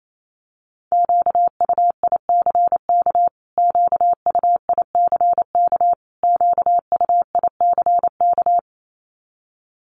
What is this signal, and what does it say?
Morse code, heard as a keyed tone, about 28 wpm, 700 Hz. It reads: QUICK QUICK QUICK
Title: Top 500 Words Repeated 28wpm